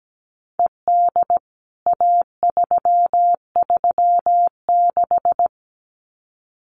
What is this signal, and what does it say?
Morse code, heard as a keyed tone, about 17 wpm, 700 Hz. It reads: ED A336
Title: CWT Exchange 17wpm